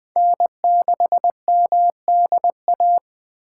Morse code, heard as a keyed tone, about 20 wpm, 700 Hz. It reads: N6MDA
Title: USA Call Signs - 5 Characters - 6x Character Spacing - 20wpm